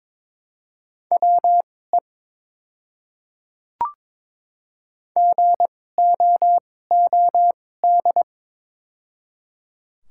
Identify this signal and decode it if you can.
Morse code, heard as a keyed tone, about 22 wpm, spaced as for 16 wpm, 700 Hz. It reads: WE GOOD
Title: 2x Character Spacing - Top 200 Words - 22wpm